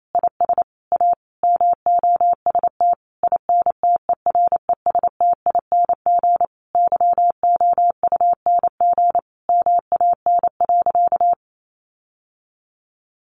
Morse code, heard as a keyed tone, about 28 wpm, 700 Hz. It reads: IS A MOST INTERESTING YOUNG MAN.
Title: Sentences from Top 1000 Words 28wpm